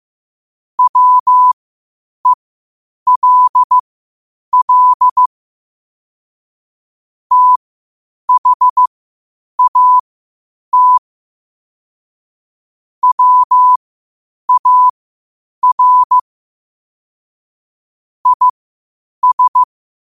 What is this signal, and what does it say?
Morse code, heard as a keyed tone, about 15 wpm, spaced as for 8 wpm, 1000 Hz. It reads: WELL THAT WAR IS
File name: qotdfiles / 8wpm